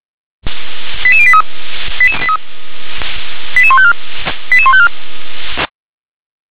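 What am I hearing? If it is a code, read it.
Five-tone selective call (ZVEI), heard as quick runs of five tones, tones of about 70 ms. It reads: 88083 88154 88154
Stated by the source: Five-tone ZVEI selective call received by SDR.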